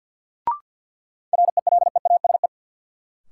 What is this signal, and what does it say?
Morse code, heard as a keyed tone, about 52 wpm, 700 Hz. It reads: RELEASE
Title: Sentences from Top 1100 Words - New Word Drill - 50wpm